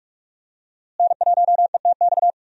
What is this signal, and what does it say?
Morse code, heard as a keyed tone, about 41 wpm, 700 Hz. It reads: N1ETX
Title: USA Call Signs - 5 Characters - Repeated 3x - 45wpm